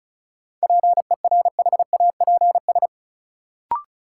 Morse code, heard as a keyed tone, about 36 wpm, 700 Hz. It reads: PERHAPS
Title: Sentences from Top 500 Words - New Word Drill - 35wpm